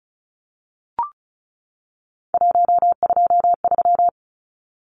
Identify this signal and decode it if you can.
Morse code, heard as a keyed tone, about 34 wpm, 700 Hz. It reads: 123
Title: Numbers - 3-Digit 35wpm